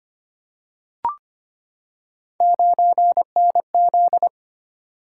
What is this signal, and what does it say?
Morse code, heard as a keyed tone, about 25 wpm, 700 Hz. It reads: 9NZ
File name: Three Letter-Number Combinations - 25wpm